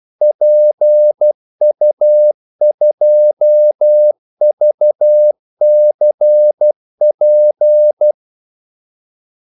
Morse code, heard as a keyed tone, about 12 wpm, 600 Hz. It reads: PU2VCP
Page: 2 ekstra ordmellemrum mellem kaldesignalerne